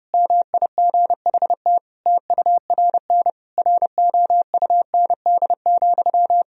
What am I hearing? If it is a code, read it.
Morse code, heard as a keyed tone, about 32 wpm, 700 Hz. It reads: MIGHT TURN ROUND,
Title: Sentences from Top 1000 Words - Review Part 1 - 30wpm